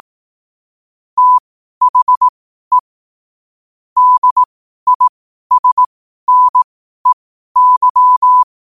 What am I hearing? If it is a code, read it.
Morse code, heard as a keyed tone, about 18 wpm, spaced as for 12 wpm, 1000 Hz. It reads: THE DISNEY
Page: Quotes for Fri, 15 Aug 2025 in Morse Code at 12 words per minute.